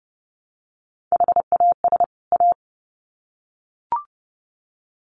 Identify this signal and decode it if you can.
Morse code, heard as a keyed tone, about 30 wpm, 700 Hz. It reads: HAS A
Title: Sets of 2 Words 30wpm